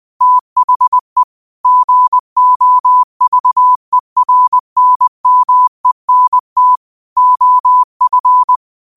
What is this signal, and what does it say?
Morse code, heard as a keyed tone, about 20 wpm, 1000 Hz. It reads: THE GOVERNMENT OF
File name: qotdfiles / 20wpm